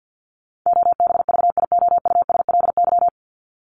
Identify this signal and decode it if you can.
Morse code, heard as a keyed tone, about 52 wpm, 700 Hz. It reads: OBVIOUSLY
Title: LY Suffix - Top 100 Words with Base Words 50wpm